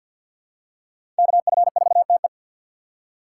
Morse code, heard as a keyed tone, about 47 wpm, 700 Hz. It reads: KF4TE
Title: USA Call Signs - 5 Characters - 2x Character Spacing - 50wpm